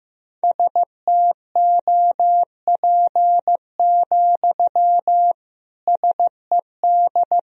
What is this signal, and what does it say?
Morse code, heard as a keyed tone, about 15 wpm, 700 Hz. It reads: STOP, SED
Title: Sentences from Top 300 Words and Abbreviations 15wpm